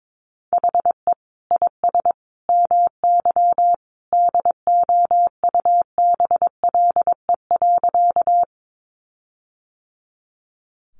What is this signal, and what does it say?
Morse code, heard as a keyed tone, about 22 wpm, 700 Hz. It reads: HE IS MY DOUBLE.
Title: Sentences from Top 1400 Words 22wpm